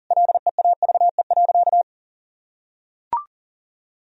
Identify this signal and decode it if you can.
Morse code, heard as a keyed tone, about 40 wpm, 700 Hz. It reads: LEAVE.
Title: Sentences from Top 1000 Words - Review Part 4 - 40wpm